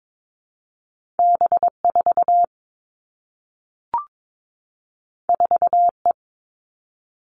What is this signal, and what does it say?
Morse code, heard as a keyed tone, about 22 wpm, 700 Hz. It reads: B 4 4 E
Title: Two Challenging Letter-Number Combinations 22wpm